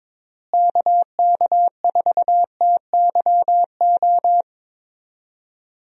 Morse code, heard as a keyed tone, about 22 wpm, 700 Hz. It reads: KK4TYO
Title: USA Call Signs - 6 Characters - Repeated 3x - 22wpm